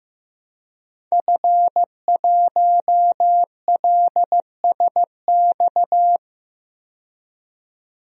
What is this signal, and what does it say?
Morse code, heard as a keyed tone, about 15 wpm, 700 Hz. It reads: F1LSX
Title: DXCC-100 Call Signs - 5 Characters - 15wpm